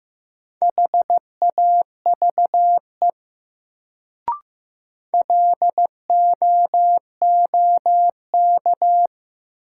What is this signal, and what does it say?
Morse code, heard as a keyed tone, about 15 wpm, 700 Hz. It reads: HAVE LOOK
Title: Sentences from Top 100 Words - New Word Drill - 15wpm